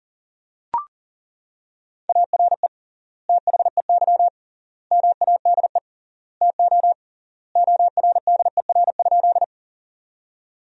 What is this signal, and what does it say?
Morse code, heard as a keyed tone, about 40 wpm, 700 Hz. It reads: ARE THEY MADE TO ORDER?
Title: 3x Word-Spacing - Sentences from Top 400 Words - 40wpm